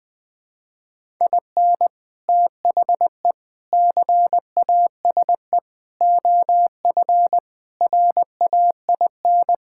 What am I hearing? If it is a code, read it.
Morse code, heard as a keyed tone, about 20 wpm, 700 Hz. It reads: IN THE CASE OF RAIN
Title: Sentences with 4-Word Phrases - 20wpm